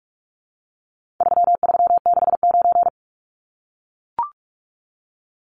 Morse code, heard as a keyed tone, about 44 wpm, 700 Hz. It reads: 3369
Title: Numbers - 4-Digit 45wpm